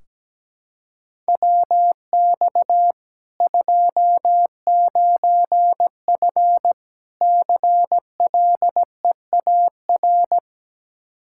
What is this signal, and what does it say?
Morse code, heard as a keyed tone, about 17 wpm, 700 Hz. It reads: WX 29F CLEAR
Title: Practice QSO - Exchange 2 - Info - 17wpm